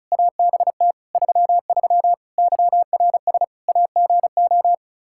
Morse code, heard as a keyed tone, about 36 wpm, 700 Hz. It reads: ABT 33 YRS AGO
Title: Practice QSO - Exchange 3 - More Info - 35wpm